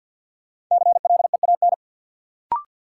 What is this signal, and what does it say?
Morse code, heard as a keyed tone, about 53 wpm, 700 Hz. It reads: CLEAN
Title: Sentences from Top 1100 Words - New Word Drill Alt - 50wpm